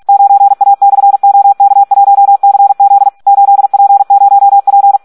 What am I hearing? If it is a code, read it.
Morse code, heard as a keyed tone, about 46 wpm, 800 Hz. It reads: 9A/OK1CZ 8P0P
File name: Audio practica CW / SD 1 46